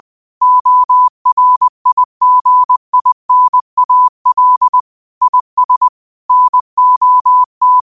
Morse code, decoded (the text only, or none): ORIGINAL IS NOT